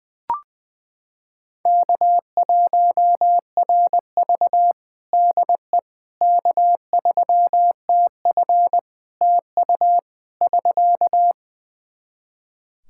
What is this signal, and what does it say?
Morse code, heard as a keyed tone, about 21 wpm, 700 Hz. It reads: K1RV DE K3TF TU <SK>
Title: Practice QSO - Exchange 4 - Ending - 20wpm